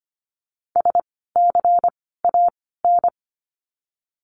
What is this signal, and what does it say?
Morse code, heard as a keyed tone, about 25 wpm, spaced as for 16 wpm, 700 Hz. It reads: SCAN
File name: course / 25:10wpm / 094 Lesson 16f - Letter c - Words 25wpm